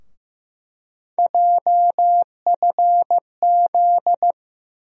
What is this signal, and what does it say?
Morse code, heard as a keyed tone, about 15 wpm, 700 Hz. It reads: JFZ
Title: USA Call Sign Postfixes - 3 Letters 15wpm